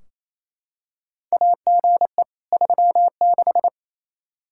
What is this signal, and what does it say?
Morse code, heard as a keyed tone, about 28 wpm, 700 Hz. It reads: AGE 36
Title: Practice QSO - Exchange 3 - More Info - 28wpm